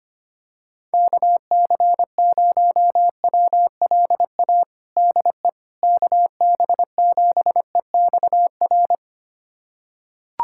Morse code, heard as a keyed tone, about 25 wpm, 700 Hz. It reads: KC0WLA DE KB7EXR
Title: USA Callsign DE Callsign 25wpm